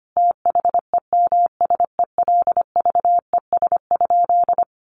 Morse code, heard as a keyed tone, about 25 wpm, 700 Hz. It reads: THEMSELVES?
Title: Sentences from Top 800 Words 25wpm